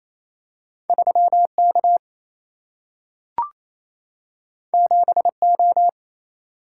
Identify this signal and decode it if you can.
Morse code, heard as a keyed tone, about 28 wpm, 700 Hz. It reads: 3K 7O
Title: Single-Letter Single-Number 28wpm